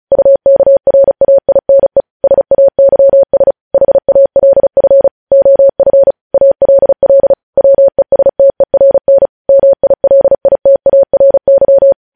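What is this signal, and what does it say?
Morse code, heard as a keyed tone, about 36 wpm, 550 Hz. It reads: UKRAINE SAYS HALF OF ALL WESTERN MILITARY
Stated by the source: Morse code news headlines practice sent at 35 wpm.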